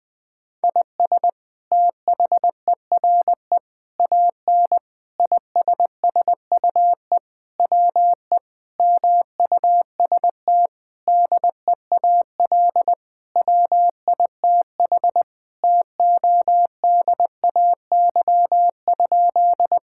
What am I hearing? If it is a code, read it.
Morse code, heard as a keyed tone, about 21 wpm, 700 Hz. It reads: IS THERE AN ISSUE WE MUST DEAL WITH TODAY?
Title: Sentences from Top 1000 Words - Review Part 6 - 20wpm